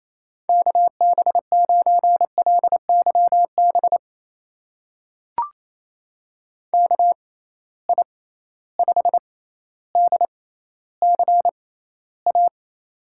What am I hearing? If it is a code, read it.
Morse code, heard as a keyed tone, about 28 wpm, spaced as for 14 wpm, 700 Hz. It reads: KB9LYB KI5DCA
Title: USA Call Signs - 6x Character Spacing - 28wpm